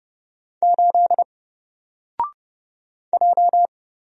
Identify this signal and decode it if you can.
Morse code, heard as a keyed tone, about 31 wpm, 700 Hz. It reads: 8 J
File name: Single-Letter-Number-30wpm